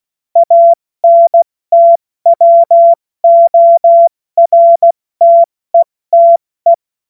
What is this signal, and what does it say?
Morse code, heard as a keyed tone, about 16 wpm, 680 Hz. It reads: ANTWORTETE
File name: Morsekurs / Witze / Witz 23. August 2022 70 F80